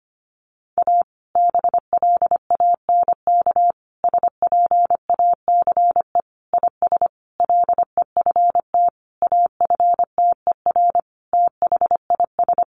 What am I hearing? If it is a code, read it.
Morse code, heard as a keyed tone, about 25 wpm, 700 Hz. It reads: A BLANK SPACE IS LEFT AFTER THIS
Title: Sentences from Top 1500 Words 25wpm